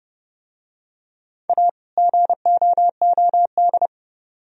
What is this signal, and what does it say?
Morse code, heard as a keyed tone, about 30 wpm, 700 Hz. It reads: A GOOD